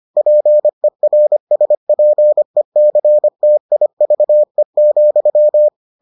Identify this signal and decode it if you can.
Morse code, heard as a keyed tone, about 25 wpm, 600 Hz. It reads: PERSPECTIVE,